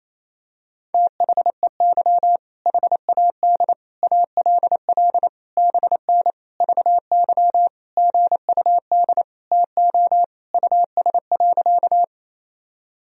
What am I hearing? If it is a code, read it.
Morse code, heard as a keyed tone, about 27 wpm, 700 Hz. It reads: THEY HAD ALL BN VY GUD TO US.
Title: Sentences from Top 200 Words and Abbreviations 28wpm